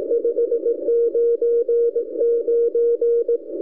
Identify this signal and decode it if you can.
Morse code, heard as a keyed tone, about 18 wpm, 470 Hz, with radio noise. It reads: 599
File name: ljud / ok1dfc